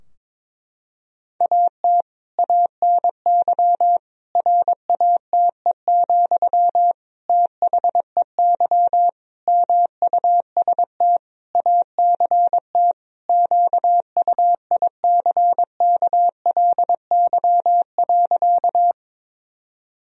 Morse code, heard as a keyed tone, about 22 wpm, 700 Hz. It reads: AT ANY RATE, THEY MUST ACT QUICKLY.
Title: Sentences from Top 1000 Words - Review Part 4 - 22wpm